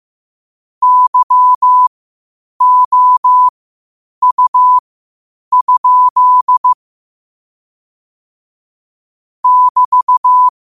Morse code, heard as a keyed tone, about 15 wpm, spaced as for 8 wpm, 1000 Hz. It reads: YOU? <BT>